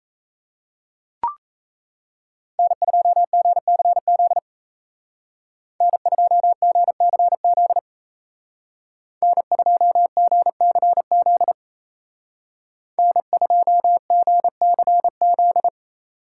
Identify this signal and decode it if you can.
Morse code, heard as a keyed tone, about 36 wpm, 700 Hz. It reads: N2GCZ N2GCZ N2GCZ N2GCZ